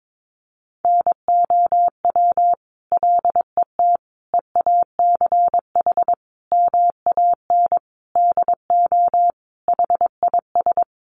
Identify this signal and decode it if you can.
Morse code, heard as a keyed tone, about 22 wpm, 700 Hz. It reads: NOW LET EACH MAN DO HIS
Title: Sentences from Top 1500 Words 22wpm